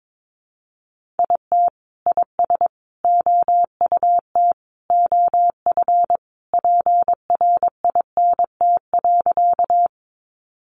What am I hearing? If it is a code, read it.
Morse code, heard as a keyed tone, about 22 wpm, 700 Hz. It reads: IT IS OUT OF PRINT.
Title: Sentences from Top 1400 Words 22wpm